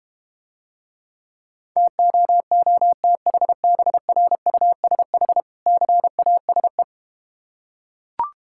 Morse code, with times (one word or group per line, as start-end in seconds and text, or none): 1.8-5.4 TOOTHBRUSH
5.7-6.8 CASE